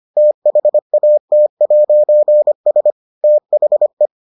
Morse code, heard as a keyed tone, about 25 wpm, 600 Hz. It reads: THAT'S THE